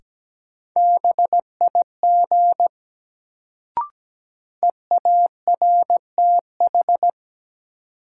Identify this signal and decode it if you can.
Morse code, heard as a keyed tone, about 17 wpm, 700 Hz. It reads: BIG EARTH
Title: Sentences from Top 200 Words - New Word Drill - 17wpm